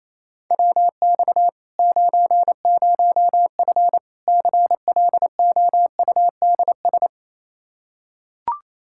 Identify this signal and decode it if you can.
Morse code, heard as a keyed tone, about 28 wpm, 700 Hz. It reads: WX 90F CLOUDS
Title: Practice QSO - Exchange 2 - Info - 28wpm